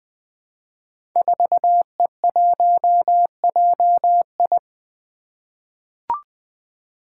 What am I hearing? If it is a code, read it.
Morse code, heard as a keyed tone, about 20 wpm, 700 Hz. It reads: 4E1JI
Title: Five Challenging Letter-Number Combinations 20wpm